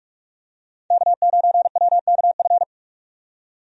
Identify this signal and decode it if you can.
Morse code, heard as a keyed tone, about 46 wpm, 700 Hz. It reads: K9WKF
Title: USA Call Signs - 5 Characters - 45wpm